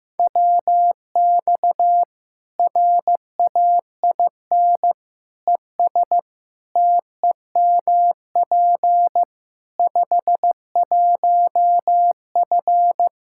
Morse code, heard as a keyed tone, about 15 wpm, 700 Hz. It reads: WX RAIN ES TEMP 51F
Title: Practice QSO - Exchange 2 - Info - 15wpm